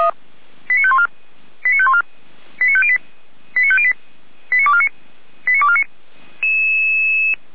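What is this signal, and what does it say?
Five-tone selective call (ZVEI), heard as quick runs of five tones, tones of about 70 ms. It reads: 89514 89514 89508 89508 89249 89249
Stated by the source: ZVEI five-tone selective call received on SDR, fire brigade Feuerwehr Orlamünde.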